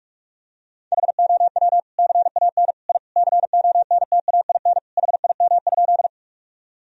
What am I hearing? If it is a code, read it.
Morse code, heard as a keyed tone, about 46 wpm, 700 Hz. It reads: HOW CAN I CONTAIN HIM?
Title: Sentences from Top 1000 Words - Review Part 2 - 45wpm